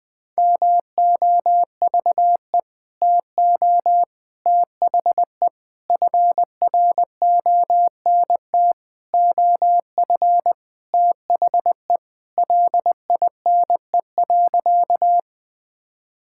Morse code, heard as a keyed tone, about 20 wpm, 700 Hz. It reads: MOVE TO THE FRONT OF THE LINE.
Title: Sentences with 5-Word Phrases - 20wpm